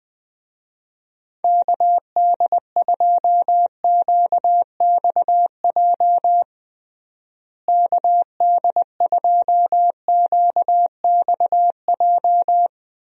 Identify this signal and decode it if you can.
Morse code, heard as a keyed tone, about 21 wpm, 700 Hz. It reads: KD2QXJ KD2QXJ
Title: USA Call Signs - 6 Characters - Repeated 2x - 20wpm